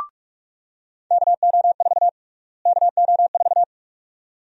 Morse code, heard as a keyed tone, about 44 wpm, 700 Hz. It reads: KO4 KO4
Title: USA Call Sign Prefixes - Repeated 2x - 45wpm